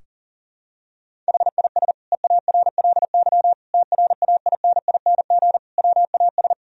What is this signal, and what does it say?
Morse code, heard as a keyed tone, about 40 wpm, 700 Hz. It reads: HIS EARLY TRAINING WAS
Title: Sentences from Top 1000 Words - Review Part 4 - 40wpm